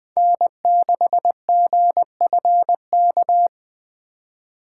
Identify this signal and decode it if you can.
Morse code, heard as a keyed tone, about 20 wpm, 700 Hz. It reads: N6GFK